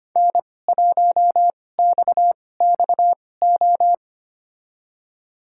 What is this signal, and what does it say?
Morse code, heard as a keyed tone, about 25 wpm, spaced as for 18 wpm, 700 Hz. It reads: N1XXO
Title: USA Call Signs - 5 Characters - 2x Character Spacing - 25wpm